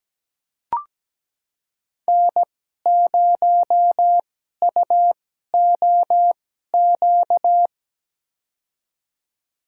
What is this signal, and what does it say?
Morse code, heard as a keyed tone, about 17 wpm, spaced as for 12 wpm, 700 Hz. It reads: N0UOQ
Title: USA Call Signs - 5 Characters - 2x Character Spacing - 17wpm